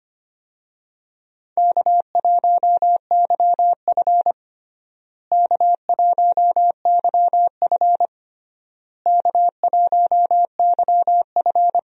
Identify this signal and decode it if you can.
Morse code, heard as a keyed tone, about 25 wpm, 700 Hz. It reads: K1YF K1YF K1YF
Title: USA Call Signs - 4 Characters - Repeated 3x - 25wpm